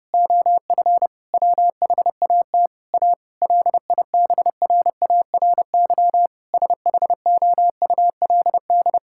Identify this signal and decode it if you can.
Morse code, heard as a keyed tone, about 30 wpm, 700 Hz. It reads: OF WHAT A LIBRARY SHOULD
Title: Sentences from Top 1000 Words - Review Part 5 - 30wpm